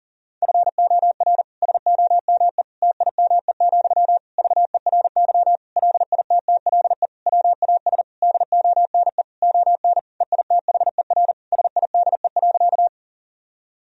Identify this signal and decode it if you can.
Morse code, heard as a keyed tone, about 40 wpm, 700 Hz. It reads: FOR SOME TIME, VERY LITTLE WAS DONE ON EITHER SIDE.
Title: Sentences from Top 1000 Words - Review Part 4 - 40wpm